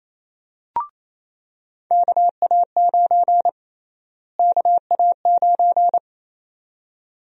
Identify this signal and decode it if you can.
Morse code, heard as a keyed tone, about 28 wpm, 700 Hz. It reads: KA9 KA9
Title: USA Call Sign Prefixes - Repeated 2x - 28wpm